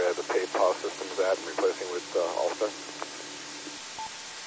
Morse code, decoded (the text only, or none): none